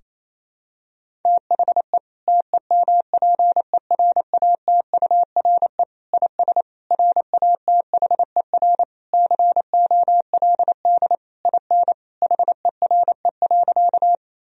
Morse code, heard as a keyed tone, about 27 wpm, 700 Hz. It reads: THE TEMPERATURE IS RATHER COLD IN HERE.
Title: Sentences from Top 1000 Words - Review Part 4 - 28wpm